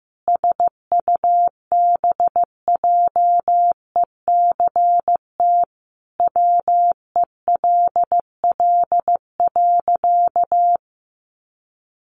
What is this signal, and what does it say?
Morse code, heard as a keyed tone, about 14 wpm, 700 Hz. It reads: SUBJECT WELL.
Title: Sentences from Top 800 Words 15wpm